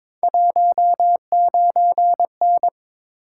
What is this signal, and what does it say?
Morse code, heard as a keyed tone, about 22 wpm, 700 Hz. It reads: 19N
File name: practice / 22wpm / Three Letter-Number Combinations - 22wpm